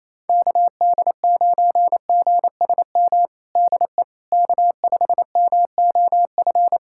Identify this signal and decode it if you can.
Morse code, heard as a keyed tone, about 28 wpm, 700 Hz. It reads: KD9GSM DE K5MOF
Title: USA Callsign DE Callsign 28wpm